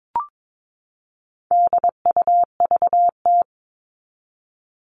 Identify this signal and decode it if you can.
Morse code, heard as a keyed tone, about 22 wpm, 700 Hz. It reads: DUVT
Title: All Cut Numbers - 4-Digit 22wpm